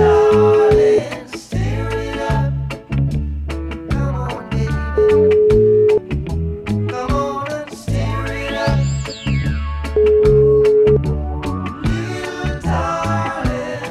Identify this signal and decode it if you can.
Ringback tone (Europe, 425 Hz), with noise.